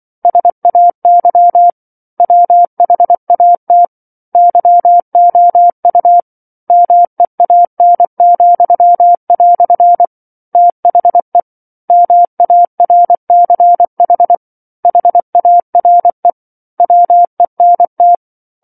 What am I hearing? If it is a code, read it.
Morse code, heard as a keyed tone, about 24 wpm, 700 Hz. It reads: SAY WHAT YOU MEAN," THE MARCH HARE WENT
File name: 07-Chapter-07 24 WPM